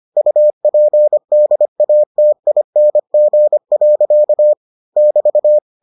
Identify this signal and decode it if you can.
Morse code, heard as a keyed tone, about 25 wpm, 600 Hz. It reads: UPDATING. <BT>